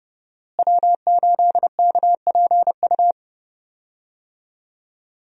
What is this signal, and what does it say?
Morse code, heard as a keyed tone, about 29 wpm, 700 Hz. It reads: W8KPU
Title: USA Call Signs - 5 Characters - Repeated 2x - 30wpm